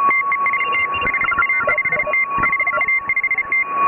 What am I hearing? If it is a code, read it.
Morse code, heard as a keyed tone, about 34 wpm, 2100 Hz, with noise. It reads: TEST HA5T HA5T
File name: AUDIO2018 / HA3FMR CW